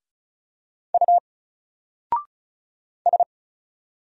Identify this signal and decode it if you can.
Morse code, heard as a keyed tone, about 34 wpm, 700 Hz. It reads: U S
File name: Single-Letter-Number-35wpm